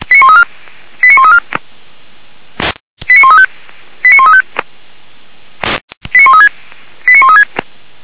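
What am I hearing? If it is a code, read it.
Five-tone selective call (ZVEI), heard as quick runs of five tones, tones of about 70 ms. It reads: 89134 89135 89135 89136 89136
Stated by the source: ZVEI five-tone call 89131 received via SDR.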